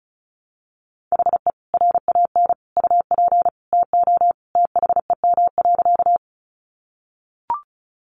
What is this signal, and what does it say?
Morse code, heard as a keyed tone, about 34 wpm, 700 Hz. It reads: HE RAN UP TO THEM.
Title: Sentences from Top 500 Words 35wpm